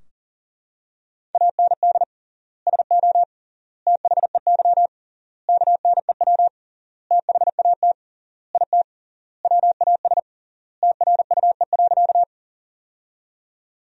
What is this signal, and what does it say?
Morse code, heard as a keyed tone, about 40 wpm, 700 Hz. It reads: AND SO THEY KNEW THAT IT WAS TRUE.
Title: 3x Word-Spacing - Sentences from Top 400 Words - 40wpm